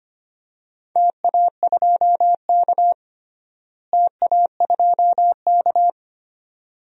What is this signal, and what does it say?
Morse code, heard as a keyed tone, about 25 wpm, 700 Hz. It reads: TA2K TA2K
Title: DXCC-100 Call Signs - 4 Characters - Repeated 2x - 25wpm